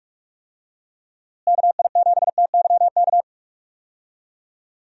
Morse code, heard as a keyed tone, about 45 wpm, 700 Hz. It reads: KI7TYK
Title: USA Call Signs - 6 Characters - Repeated 2x - 45wpm